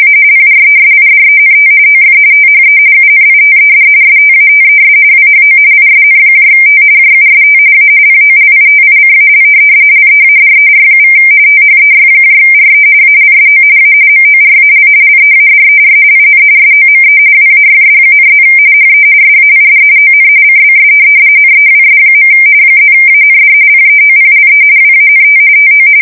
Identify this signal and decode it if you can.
RTTY, about 45.45 baud. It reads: A PRESIDENT OF THE UNITED STATES WHO ISNT JUST A POLITICAL LEADER= THEYRE ALSO A MORAL LEADER IN SOME WAYS,+ HE SAID ON CBS NEWS. +AND PEOPLE WANT TO
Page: 170 Shift 2125/2295